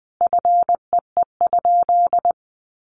Morse code, heard as a keyed tone, about 20 wpm, 700 Hz. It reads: FEE?
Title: Sentences from Top 1500 Words 20wpm